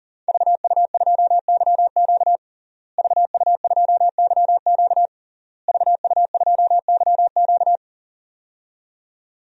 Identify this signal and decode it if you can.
Morse code, heard as a keyed tone, about 37 wpm, 700 Hz. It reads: VU2YQ VU2YQ VU2YQ
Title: DXCC-100 Call Signs - 5 Characters - Repeated 3x - 40wpm